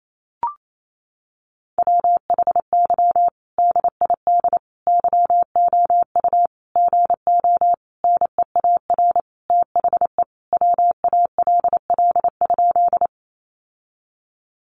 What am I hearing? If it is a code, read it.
Morse code, heard as a keyed tone, about 28 wpm, 700 Hz. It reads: WHY DID YOU GO NEAR THE WALL?
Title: Sentences from Top 1100 Words 28wpm